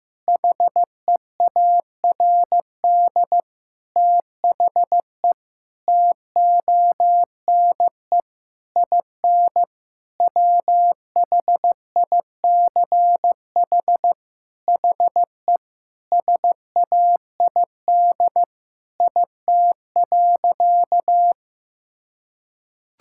Morse code, heard as a keyed tone, about 15 wpm, 700 Hz. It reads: HEARD THE TONE IN WHICH HE SAID IT.
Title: Sentences from Top 1600 Words 15wpm